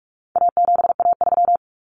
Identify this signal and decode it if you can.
Morse code, heard as a keyed tone, about 44 wpm, 700 Hz. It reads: A7A3
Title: Common Cut Numbers - 4-Digit 45wpm